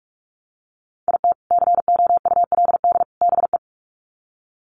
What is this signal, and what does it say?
Morse code, heard as a keyed tone, about 44 wpm, 700 Hz. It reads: IT COULD BE
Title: Sets of 3 Words 45wpm